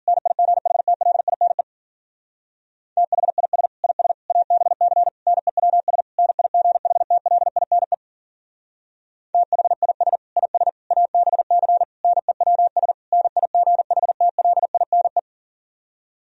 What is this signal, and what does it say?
Morse code, heard as a keyed tone, about 52 wpm, 700 Hz. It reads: NIGHTLINE THIS IS ABC NEWS NIGHTLINE THIS IS ABC NEWS NIGHTLINE